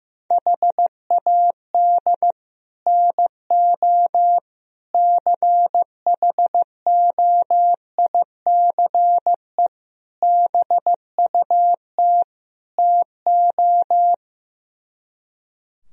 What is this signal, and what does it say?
Morse code, heard as a keyed tone, about 15 wpm, 700 Hz. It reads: HAD NO CHOICE BUT TO
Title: Speed-Racing - Sets of 5 Words 15wpm